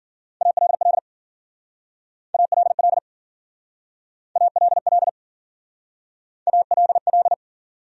Morse code, heard as a keyed tone, about 50 wpm, 700 Hz. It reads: ALL ALL ALL ALL